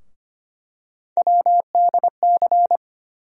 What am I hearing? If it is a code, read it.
Morse code, heard as a keyed tone, about 25 wpm, 700 Hz. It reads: WDC
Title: USA Call Sign Postfixes - 3 Letters 25wpm